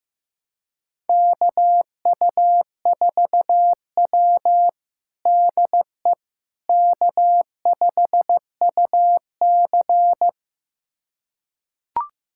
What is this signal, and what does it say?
Morse code, heard as a keyed tone, about 15 wpm, 700 Hz. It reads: KU4W DE K5UC
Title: USA Callsign DE Callsign - 4 Characters - 15wpm